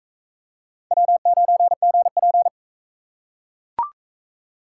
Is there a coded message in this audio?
Yes: Morse code from 0.9 to 2.5 s, heard as a keyed tone, about 39 wpm, 700 Hz. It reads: W9GP